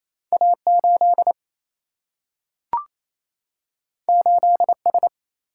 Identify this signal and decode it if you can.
Morse code, heard as a keyed tone, about 28 wpm, 700 Hz. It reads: A8 8S
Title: Single-Letter Single-Number 28wpm